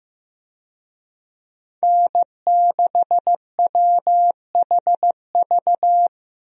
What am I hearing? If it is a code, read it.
Morse code, heard as a keyed tone, about 16 wpm, 700 Hz. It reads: N6WHV
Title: USA Call Signs - 5 Characters - Repeated 2x - 15wpm